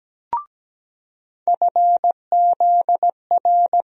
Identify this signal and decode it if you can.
Morse code, heard as a keyed tone, about 17 wpm, 700 Hz. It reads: FZR
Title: USA Call Sign Postfixes - 3 Letters 17wpm